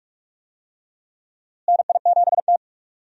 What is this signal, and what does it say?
Morse code, heard as a keyed tone, about 46 wpm, 700 Hz. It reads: NI7T
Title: USA Call Signs - 4 Characters - Repeated 3x - 45wpm